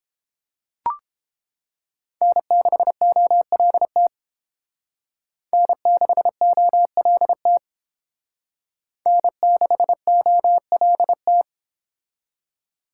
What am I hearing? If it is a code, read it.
Morse code, heard as a keyed tone, about 28 wpm, 700 Hz. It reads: N6OLT N6OLT N6OLT